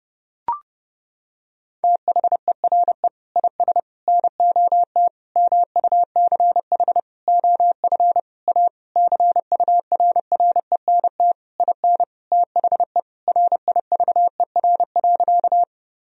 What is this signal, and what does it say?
Morse code, heard as a keyed tone, about 30 wpm, 700 Hz. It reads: THERE IS NOT MUCH OF A CURRENT IN THE RIVER.
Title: Sentences from Top 1000 Words - Review Part 4 - 30wpm